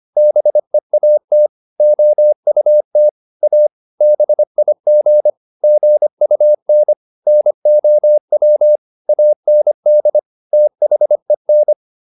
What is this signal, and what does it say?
Morse code, heard as a keyed tone, about 25 wpm, 600 Hz. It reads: BEAT OUT A BIG GUN NOW AND THEN
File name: SS305 25